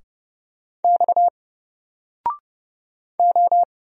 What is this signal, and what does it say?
Morse code, heard as a keyed tone, about 29 wpm, 700 Hz. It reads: X O
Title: Single Letter-Number Alt 30wpm